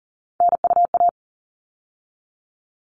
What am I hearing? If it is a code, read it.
Morse code, heard as a keyed tone, about 40 wpm, 700 Hz. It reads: NUA
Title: All Cut Numbers - 3-Digit 40wpm